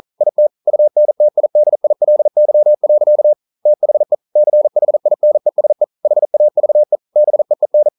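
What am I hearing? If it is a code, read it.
Morse code, heard as a keyed tone, about 42 wpm, 600 Hz. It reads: IT UNTIDILY. THE CHINESE HAVE BEEN
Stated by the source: Morse code practice at 41 wpm, from CW med Gnister.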